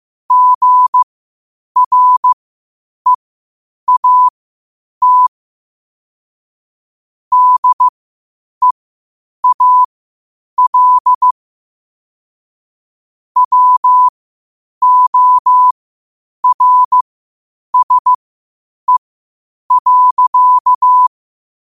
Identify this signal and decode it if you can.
Morse code, heard as a keyed tone, about 15 wpm, spaced as for 8 wpm, 1000 Hz. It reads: GREAT DEAL WORSE.
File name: qotdfiles / 8wpm